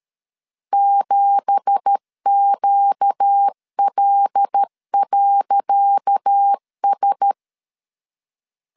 Morse code, heard as a keyed tone, about 12 wpm, 800 Hz. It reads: 7QL.S
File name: LEZIONI / 10 SIGNS / 10C DOT 12wpm 2451